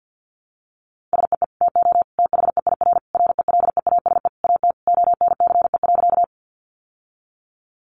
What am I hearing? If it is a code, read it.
Morse code, heard as a keyed tone, about 50 wpm, 700 Hz. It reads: SEE TO THEIR RELEASE AT ONCE.
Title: Sentences from Top 1100 Words 50wpm